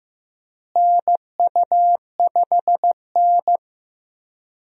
Morse code, heard as a keyed tone, about 15 wpm, 700 Hz. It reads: NU5N
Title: USA Call Signs - 4 Characters - Repeated 3x - 15wpm